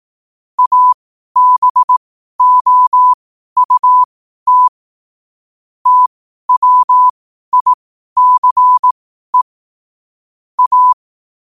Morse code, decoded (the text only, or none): ABOUT TWICE A